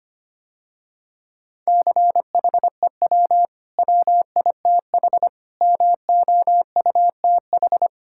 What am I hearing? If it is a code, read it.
Morse code, heard as a keyed tone, about 25 wpm, 700 Hz. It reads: CHEW WITH MOUTH
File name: Easy Sentences with Hints - 25wpm